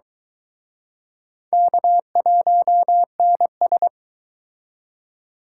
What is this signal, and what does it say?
Morse code, heard as a keyed tone, about 23 wpm, 700 Hz. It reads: K1NS